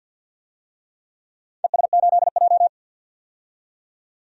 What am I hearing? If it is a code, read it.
Morse code, heard as a keyed tone, about 45 wpm, 700 Hz. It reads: ES8J
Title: DXCC-100 Call Signs - 4 Characters - 50wpm